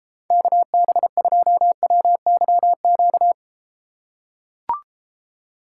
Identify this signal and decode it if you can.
Morse code, heard as a keyed tone, about 31 wpm, 700 Hz. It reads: KB2WYQ